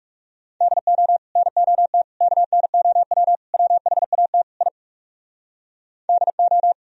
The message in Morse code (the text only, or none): DO NOT KNOW WHAT I DO